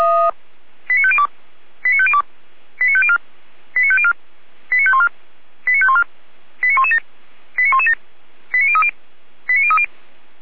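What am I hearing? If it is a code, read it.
Five-tone selective call (ZVEI), heard as quick runs of five tones, tones of about 70 ms. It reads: 89502 89502 89504 89504 89514 89514 89117 89117 89030 89030
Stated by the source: ZVEI five-tone selective call received on an SDR.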